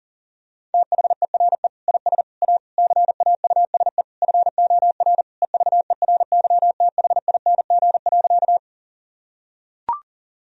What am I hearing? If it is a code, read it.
Morse code, heard as a keyed tone, about 40 wpm, 700 Hz. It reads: THERE IS A CAUSE FOR EVERYTHING.
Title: Sentences from Top 1000 Words - Review Part 1 - 40wpm